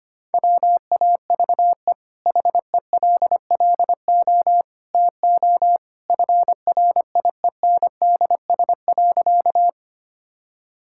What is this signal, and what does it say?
Morse code, heard as a keyed tone, about 25 wpm, 700 Hz. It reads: WAVE HELLO TO FRIENDS.
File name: Easy Sentences with Hints - 25wpm